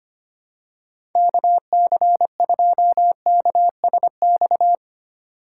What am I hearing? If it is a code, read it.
Morse code, heard as a keyed tone, about 25 wpm, 700 Hz. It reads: KC2KSX
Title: USA Call Signs - 3x Character Spacing - 25wpm